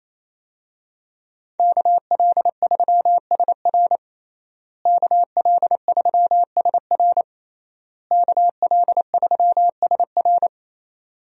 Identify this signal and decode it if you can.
Morse code, heard as a keyed tone, about 28 wpm, 700 Hz. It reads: KL3SR KL3SR KL3SR
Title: USA Call Signs - 5 Characters - Repeated 3x - 28wpm